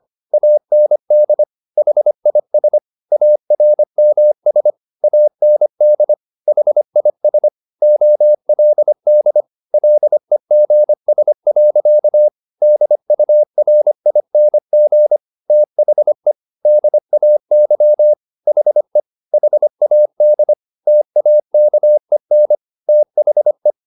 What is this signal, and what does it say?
Morse code, heard as a keyed tone, about 25 wpm, 600 Hz. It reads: AND HIS ARMS AND HIS OLD LEGS. DURING THE DAY HE HAD TAKEN THE
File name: the old man and the sea part 1 25wpm